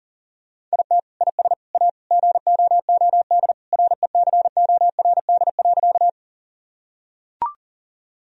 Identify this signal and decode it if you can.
Morse code, heard as a keyed tone, about 37 wpm, 700 Hz. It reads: IT IS A GOOD RECORD.
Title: Sentences from Top 1000 Words - Review Part 3 - 40wpm